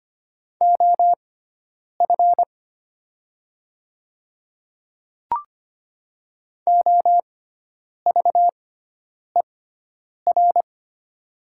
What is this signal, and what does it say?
Morse code, heard as a keyed tone, about 25 wpm, spaced as for 9 wpm, 690 Hz. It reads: OF OVER
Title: 6x Character Spacing - Top 100 Words - 25wpm